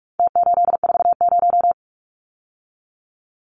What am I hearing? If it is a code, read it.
Morse code, heard as a keyed tone, about 44 wpm, 700 Hz. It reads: T840
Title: Numbers - Contest Style - 4-Digit 45wpm